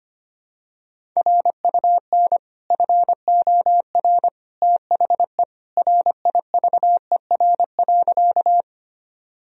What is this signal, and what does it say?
Morse code, heard as a keyed tone, about 25 wpm, 700 Hz. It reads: RUN FOR THE RIVER.
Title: Sentences from Top 1000 Words - Review Part 1 - 25wpm